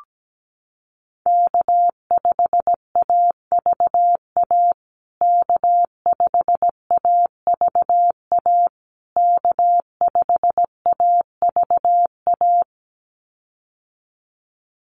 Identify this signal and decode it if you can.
Morse code, heard as a keyed tone, about 17 wpm, 700 Hz. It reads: K5AVA K5AVA K5AVA
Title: USA Call Signs Repeated 3x 17wpm